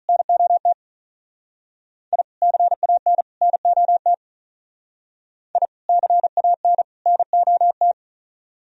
Morse code, heard as a keyed tone, about 42 wpm, 700 Hz. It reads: NOT I CAN NOT I CAN NOT
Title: Speed-Racing - Sets of 3 Words 35wpm